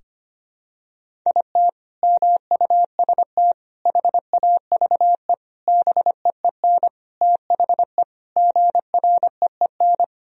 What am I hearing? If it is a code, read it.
Morse code, heard as a keyed tone, about 25 wpm, 700 Hz. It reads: IT MUST HAVE BEEN THE GREEN
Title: Sentences from Top 1000 Words - Review Part 2 - 25wpm